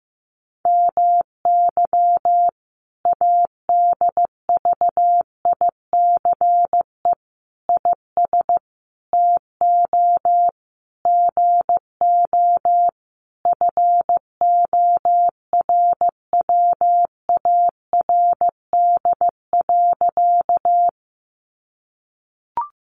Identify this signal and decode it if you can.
Morse code, heard as a keyed tone, about 15 wpm, 700 Hz. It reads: MY ADVICE IS TO GO FORWARD.
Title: Sentences from Top 1300 Words 15wpm